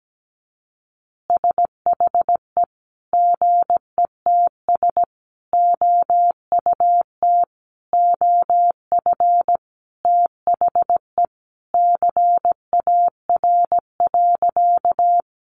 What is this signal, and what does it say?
Morse code, heard as a keyed tone, about 17 wpm, 700 Hz. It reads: SHE GETS OUT OF THE CAR.
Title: Easy Phrases 17wpm